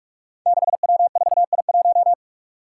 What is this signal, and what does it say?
Morse code, heard as a keyed tone, about 44 wpm, 700 Hz. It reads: 6W4I1
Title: Five Challenging Letter-Number Combinations 45wpm